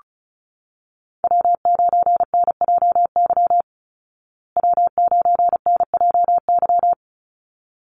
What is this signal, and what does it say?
Morse code, heard as a keyed tone, about 34 wpm, 700 Hz. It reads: W9NJY W9NJY
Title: Call Signs Repeated 3x - 35wpm